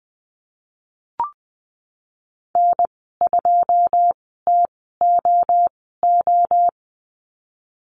Morse code, heard as a keyed tone, about 20 wpm, spaced as for 15 wpm, 700 Hz. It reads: N2TOO
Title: 122 Lesson 20g - Number 3 - Call Signs 20wpm